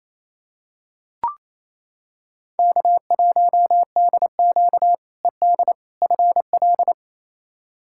Morse code, heard as a keyed tone, about 28 wpm, 700 Hz. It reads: K1DQ ED FL
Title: K1USN SST QSO Practice - 28wpm